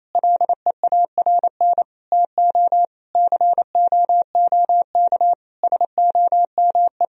Morse code, heard as a keyed tone, about 27 wpm, 700 Hz. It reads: LEARN TO COOK SOME
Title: Sentences from Top 1000 Words - Review Part 4 - 28wpm